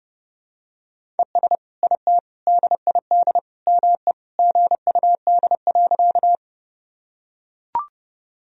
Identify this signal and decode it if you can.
Morse code, heard as a keyed tone, about 31 wpm, 700 Hz. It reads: ES IT DID ME GUD.
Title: Sentences from Top 200 Words and Abbreviations 30wpm